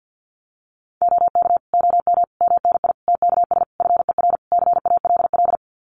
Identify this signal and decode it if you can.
Morse code, heard as a keyed tone, about 48 wpm, 700 Hz. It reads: OK OM MNI TKS FER CALL
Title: Common QSO Phrases 50wpm